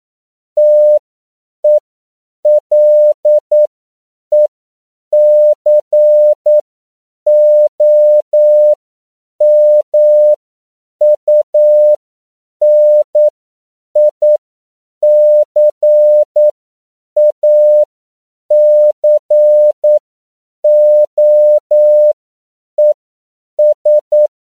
Morse code, heard as a keyed tone, about 9 wpm, spaced as for 7 wpm, 600 Hz. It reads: TELECOMUNICACOES